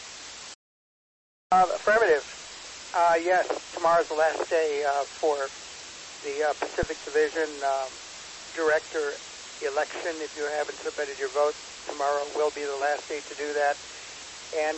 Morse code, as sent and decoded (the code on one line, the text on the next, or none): none